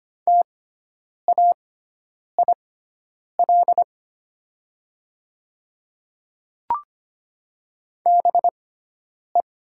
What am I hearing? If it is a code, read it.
Morse code, heard as a keyed tone, about 25 wpm, spaced as for 9 wpm, 700 Hz. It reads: TAIL BE
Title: 6x Character Spacing - Top 500 Words - 25wpm